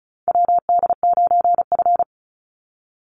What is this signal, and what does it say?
Morse code, heard as a keyed tone, about 34 wpm, 700 Hz. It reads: WD9F
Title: NA SOTA Chaser Callsigns 35wpm